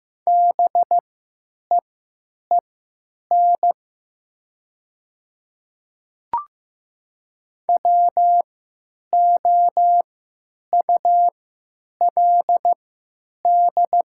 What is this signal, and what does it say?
Morse code, heard as a keyed tone, about 15 wpm, spaced as for 9 wpm, 700 Hz. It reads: BEEN WOULD
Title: 3x Character Spacing - Top 200 Words - 15wpm